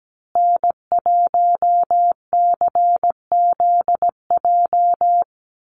Morse code, heard as a keyed tone, about 17 wpm, 700 Hz. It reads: N1CZJ
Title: USA Call Signs Repeated 3x 17wpm